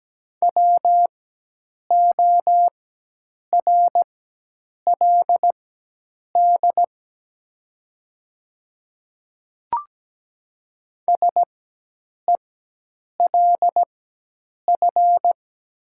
Morse code, heard as a keyed tone, about 17 wpm, spaced as for 8 wpm, 700 Hz. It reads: WORLD SELF
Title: 4x Character Spacing - Top 300 Words - 17wpm